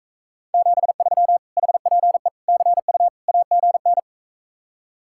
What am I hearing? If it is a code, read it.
Morse code, heard as a keyed tone, about 46 wpm, 700 Hz. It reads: 73 HPE CU AGN